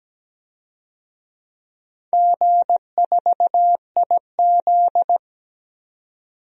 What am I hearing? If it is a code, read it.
Morse code, heard as a keyed tone, about 17 wpm, 700 Hz. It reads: G4IZ
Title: DXCC-100 Call Signs - 4 Characters - 17wpm